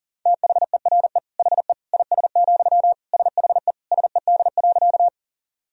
Morse code, heard as a keyed tone, about 40 wpm, 700 Hz. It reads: THERE HE IS, SHE SED.
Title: Sentences from Top 100 Words and Abbreviations 40wpm